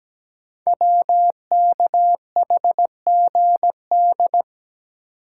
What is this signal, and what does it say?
Morse code, heard as a keyed tone, about 17 wpm, 700 Hz. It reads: WKHGD
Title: Code Groups - 5 Letters - 17wpm